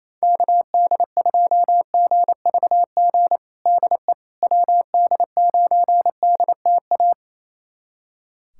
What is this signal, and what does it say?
Morse code, heard as a keyed tone, about 28 wpm, 700 Hz. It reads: KD2GVG DE WD9DTA
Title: USA Callsign DE Callsign - 6 Characters - 28wpm